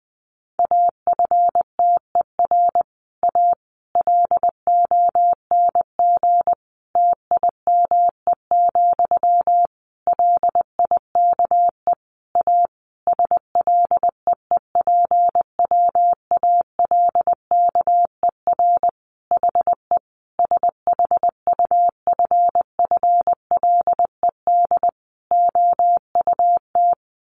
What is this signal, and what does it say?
Morse code, heard as a keyed tone, about 20 wpm, 700 Hz. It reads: AFTER A LONG TIME, LIKE A SLEEPWALKER HE SHUFFLED OUT
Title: A Sound Of Thunder 20wpm